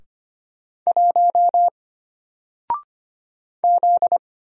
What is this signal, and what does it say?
Morse code, heard as a keyed tone, about 25 wpm, 700 Hz. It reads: 1 Z
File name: Single-Letter-Number-25wpm